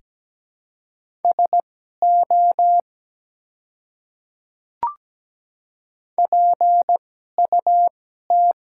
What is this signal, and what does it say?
Morse code, heard as a keyed tone, about 17 wpm, spaced as for 12 wpm, 700 Hz. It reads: SO PUT
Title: 2x Character Spacing - Top 200 Words - 17wpm